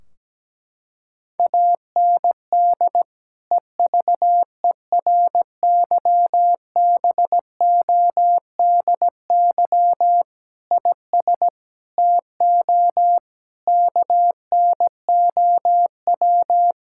Morse code, heard as a keyed tone, about 17 wpm, 700 Hz. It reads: AND EVERYBODY IS TO KNOW
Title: Sentences from Top 1600 Words 17wpm